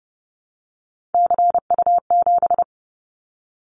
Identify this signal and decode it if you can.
Morse code, heard as a keyed tone, about 30 wpm, 700 Hz. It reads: CU7
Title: Callsign Prefixes for Top 100 DXCC - 30wpm